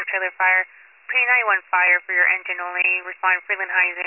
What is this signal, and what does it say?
Emergency Alert System (SAME) tones, heard as harsh buzzing bursts, with speech. It is heard from 0.3 to 3.9 s.